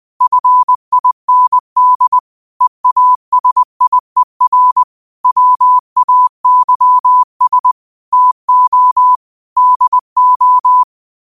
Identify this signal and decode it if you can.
Morse code, heard as a keyed tone, about 20 wpm, 1000 Hz. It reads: FIND EASIER WAYS TO DO
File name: qotdfiles / 20wpm